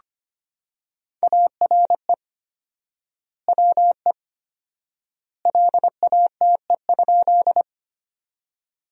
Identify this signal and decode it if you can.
Morse code, heard as a keyed tone, about 25 wpm, 700 Hz. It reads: ARE WE LATE?
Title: 4x Word-Spacing - Sentences from Top 300 Words - 25wpm